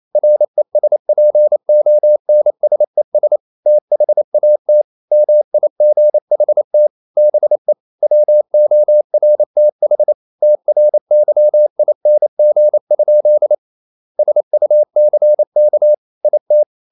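Morse code, heard as a keyed tone, about 28 wpm, 600 Hz. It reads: RESPONSES THAT MIGHT BE WORTH TRYING? SUCK IT
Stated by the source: Morse code practice at 28 wpm, text of Ken Follett's novel Never in English, from CW med Gnister.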